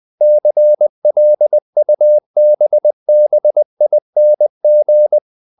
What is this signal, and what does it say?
Morse code, heard as a keyed tone, about 20 wpm, 600 Hz. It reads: CLUBBING
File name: the old man and the sea / the old man and the sea part 1 20wpm